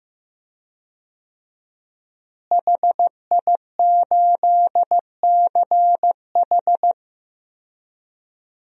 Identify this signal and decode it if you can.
Morse code, heard as a keyed tone, about 15 wpm, 700 Hz. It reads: HI8CH
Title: DXCC-100 Call Signs - 5 Characters - Repeated 3x - 15wpm